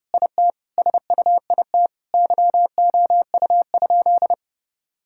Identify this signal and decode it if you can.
Morse code, heard as a keyed tone, about 30 wpm, 700 Hz. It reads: IT SUIT YOU?
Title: Sentences from Top 1600 Words 30wpm